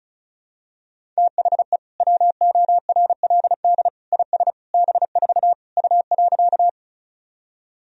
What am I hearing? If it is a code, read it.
Morse code, heard as a keyed tone, about 36 wpm, 700 Hz. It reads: THE WORLD IS B4 U.
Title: Sentences from Top 200 Words and Abbreviations 35wpm